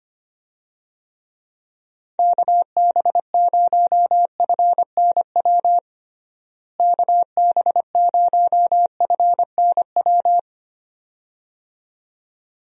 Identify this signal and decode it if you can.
Morse code, heard as a keyed tone, about 25 wpm, 700 Hz. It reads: KB0FNW KB0FNW
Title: USA Call Signs - 6 Characters - Repeated 2x - 25wpm